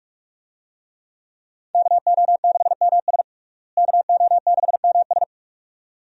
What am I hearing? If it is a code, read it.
Morse code, heard as a keyed tone, about 42 wpm, 700 Hz. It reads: KO6MS KO6MS
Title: USA Call Signs - 5 Characters - Repeated 2x - 45wpm